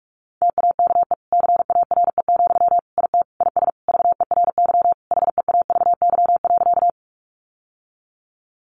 Morse code, heard as a keyed tone, about 44 wpm, 700 Hz. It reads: TAKE CARE, IT IS VERY HEAVY.
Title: Sentences from Top 1100 Words 45wpm